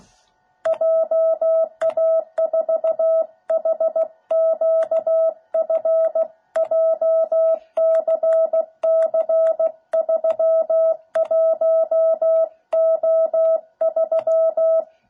Morse code, heard as a keyed tone, about 16 wpm, 660 Hz. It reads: JA4HQF JCC31O3